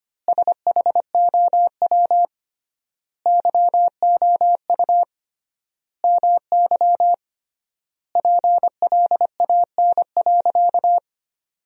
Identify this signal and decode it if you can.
Morse code, heard as a keyed tone, about 25 wpm, 700 Hz. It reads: SHOW YOU MY PLAN.
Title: 3x Word-Spacing - Sentences from Top 500 Words - 25wpm